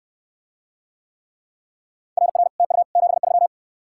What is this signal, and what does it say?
Morse code, heard as a keyed tone, about 66 wpm, 700 Hz. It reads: RR TU 73